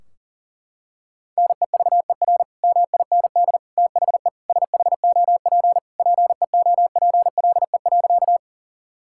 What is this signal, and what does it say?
Morse code, heard as a keyed tone, about 40 wpm, 700 Hz. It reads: NEVER MIND THE SHOP PEOPLE.
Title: Sentences from Top 1600 Words 40wpm